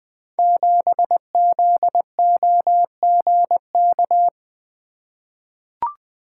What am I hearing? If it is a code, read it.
Morse code, heard as a keyed tone, about 20 wpm, 700 Hz. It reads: 7ZOGK
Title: Code Groups - 5 Letter:Numbers - 20wpm